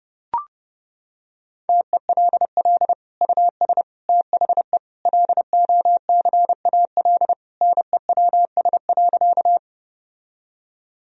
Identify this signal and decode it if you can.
Morse code, heard as a keyed tone, about 30 wpm, 700 Hz. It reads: TELL US THE LOCAL NEWS.
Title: Sentences from Top 1000 Words - Review Part 3 - 30wpm